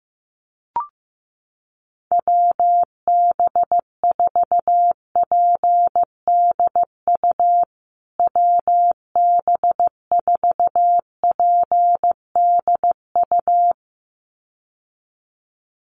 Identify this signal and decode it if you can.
Morse code, heard as a keyed tone, about 15 wpm, 700 Hz. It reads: WB4PDU WB4PDU
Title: USA Call Signs Repeated 2x 15wpm